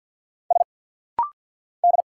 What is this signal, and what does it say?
Morse code, heard as a keyed tone, about 50 wpm, 690 Hz. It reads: S D
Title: Single Letters - Rapid-Fire 50wpm